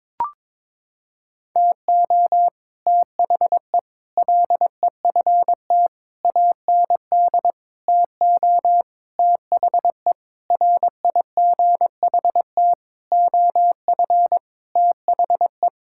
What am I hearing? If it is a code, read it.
Morse code, heard as a keyed tone, about 22 wpm, 700 Hz. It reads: TO THE LEFT AND TO THE RIGHT OF THE
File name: Sentences with 9-Word Phrases - 22wpm